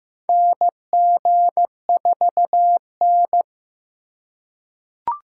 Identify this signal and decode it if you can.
Morse code, heard as a keyed tone, about 15 wpm, 700 Hz. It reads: NG4N
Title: USA Call Signs - 4 Characters - 3x Character Spacing - 15wpm